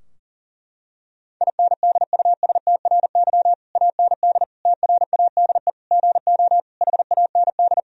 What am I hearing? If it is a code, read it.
Morse code, heard as a keyed tone, about 40 wpm, 700 Hz. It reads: INDUSTRY AND TRADE GO HAND
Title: Sentences from Top 1000 Words - Review Part 4 - 40wpm